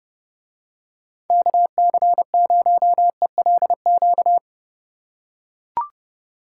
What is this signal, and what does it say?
Morse code, heard as a keyed tone, about 31 wpm, 700 Hz. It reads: KC0ELQ